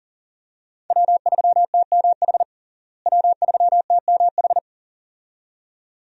Morse code, heard as a keyed tone, about 40 wpm, 700 Hz. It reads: W3TMH W3TMH
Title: USA Call Signs - 5 Characters - Repeated 2x - 40wpm